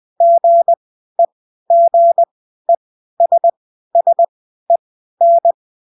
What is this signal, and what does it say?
Morse code, heard as a keyed tone, about 20 wpm, spaced as for 13 wpm, 680 Hz. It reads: GEGESSEN